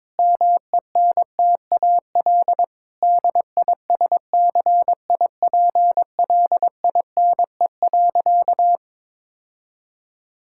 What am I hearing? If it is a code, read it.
Morse code, heard as a keyed tone, about 21 wpm, 700 Hz. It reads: MENTAL DISCIPLINE.
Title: Sentences from Top 1000 Words - Review Part 6 - 22wpm